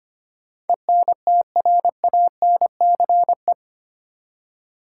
Morse code, heard as a keyed tone, about 25 wpm, 700 Hz. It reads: ENTRANCE
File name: 8 Letter Words - First 3 Letters Given - 25wpm